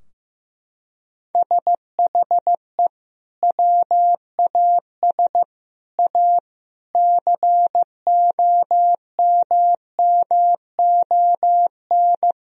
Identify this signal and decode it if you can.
Morse code, heard as a keyed tone, about 14 wpm, 700 Hz. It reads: SHE WAS A COMMON
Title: Sentences from Top 1000 Words - Review Part 4 - 15wpm